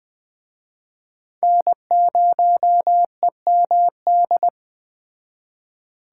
Morse code, heard as a keyed tone, about 20 wpm, 700 Hz. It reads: N0EMD
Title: Speed Racing - USA 1x3 Call Signs 17wpm